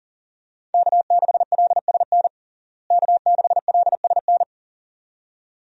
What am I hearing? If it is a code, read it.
Morse code, heard as a keyed tone, about 37 wpm, 700 Hz. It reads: K6LSN K6LSN
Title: USA Call Signs - 5 Characters - Repeated 2x - 40wpm